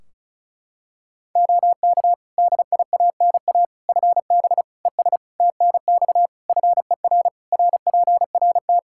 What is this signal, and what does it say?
Morse code, heard as a keyed tone, about 34 wpm, 700 Hz. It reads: OK DIANA FB ES TNX FER RPRT
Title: Practice QSO - Exchange 2 - Info - 35wpm